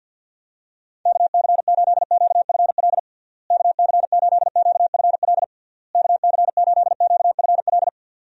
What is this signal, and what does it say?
Morse code, heard as a keyed tone, about 49 wpm, 700 Hz. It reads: KC8QFL KC8QFL KC8QFL
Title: USA Call Signs - 6 Characters - Repeated 3x - 50wpm